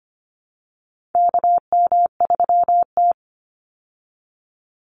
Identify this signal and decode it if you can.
Morse code, heard as a keyed tone, about 25 wpm, 700 Hz. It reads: KM3T
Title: Call Signs Repeated 2x - 25wpm